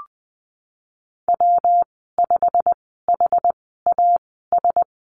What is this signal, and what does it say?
Morse code, heard as a keyed tone, about 20 wpm, spaced as for 14 wpm, 700 Hz. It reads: W5HAS
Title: 122 Lesson 20g - Number 3 - Call Signs 20wpm